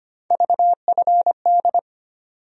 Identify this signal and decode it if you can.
Morse code, heard as a keyed tone, about 25 wpm, 700 Hz. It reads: VFD
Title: USA Call Sign Postfixes - 3 Letters 25wpm